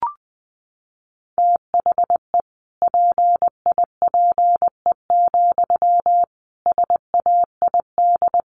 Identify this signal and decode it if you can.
Morse code, heard as a keyed tone, about 20 wpm, 700 Hz. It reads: THE PIPE, SAID
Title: Sentences from Top 1500 Words 20wpm